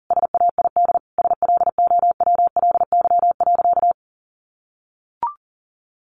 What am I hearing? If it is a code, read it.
Morse code, heard as a keyed tone, about 40 wpm, 700 Hz. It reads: SAID SLOWLY.
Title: Sentences from Top 1100 Words 40wpm